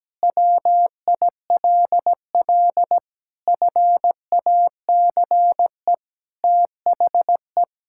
Morse code, heard as a keyed tone, about 17 wpm, 700 Hz. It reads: WILL FACE THE
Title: Sentences from Top 1000 Words - Review Part 6 - 17wpm